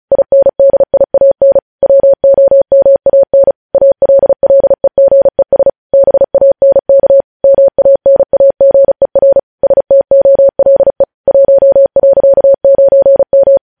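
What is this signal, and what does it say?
Morse code, heard as a keyed tone, about 36 wpm, 550 Hz. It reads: INDIAN WOMAN ALLEGES BANK MANAGER STOLE 1.9M
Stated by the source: News headlines sent in Morse at 35 WPM.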